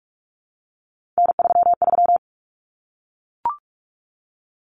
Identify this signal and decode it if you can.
Morse code, heard as a keyed tone, about 44 wpm, 700 Hz. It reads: N33
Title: Common Cut Numbers - 3-Digit 45wpm